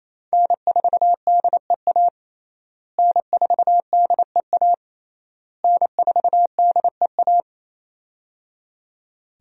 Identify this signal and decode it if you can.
Morse code, heard as a keyed tone, about 28 wpm, 700 Hz. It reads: N4DEA N4DEA N4DEA
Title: USA Call Signs - 5 Characters - Repeated 3x - 28wpm